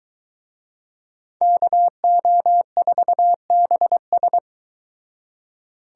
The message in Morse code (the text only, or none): KO4BS